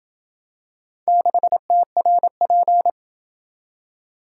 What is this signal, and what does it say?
Morse code, heard as a keyed tone, about 27 wpm, 700 Hz. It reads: 6TRP